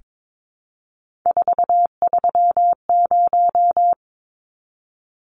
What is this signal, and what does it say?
Morse code, heard as a keyed tone, about 22 wpm, 700 Hz. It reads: 430
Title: Numbers - 3-Digit 22wpm